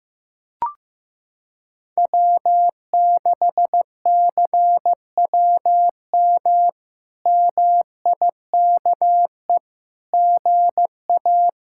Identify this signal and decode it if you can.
Morse code, heard as a keyed tone, about 16 wpm, 700 Hz. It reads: W6CWM MIKE GA
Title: K1USN SST QSO Practice - 15wpm